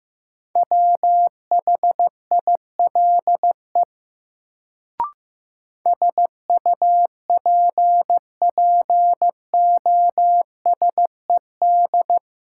Morse code, heard as a keyed tone, about 16 wpm, 700 Hz. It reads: WHILE SUPPOSED
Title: Sets of 3 Words - New Word Drill - 15wpm